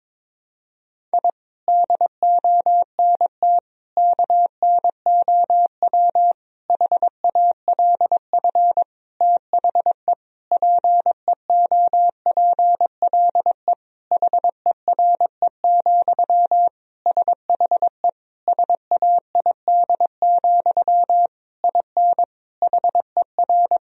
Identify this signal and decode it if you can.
Morse code, heard as a keyed tone, about 22 wpm, 700 Hz. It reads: I DONT KNOW HALF THE PEOPLE HERE, SHE SAID, IN HER
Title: Sentences from Top 1000 Words - Review Part 1 - 22wpm